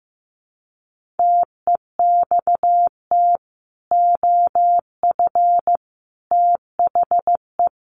Morse code, heard as a keyed tone, about 15 wpm, 700 Hz. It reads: TEXT OF THE
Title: Sentences from Top 1400 Words 15wpm